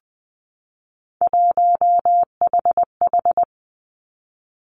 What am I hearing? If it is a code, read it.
Morse code, heard as a keyed tone, about 20 wpm, 700 Hz. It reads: 1HH
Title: Three Challenging Letter-Number Combinations 20wpm